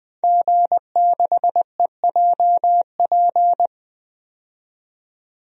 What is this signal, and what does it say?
Morse code, heard as a keyed tone, about 20 wpm, 700 Hz. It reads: G6EJP
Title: DXCC-100 Call Signs - 5 Characters - 20wpm